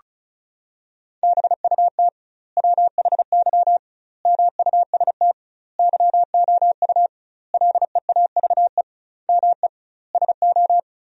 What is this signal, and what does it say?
Morse code, heard as a keyed tone, about 34 wpm, 700 Hz. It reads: BUT WHY MUST YOU LEAVE ME SO
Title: 2x Word-Spacing - Sentences from Top 400 Words - 35wpm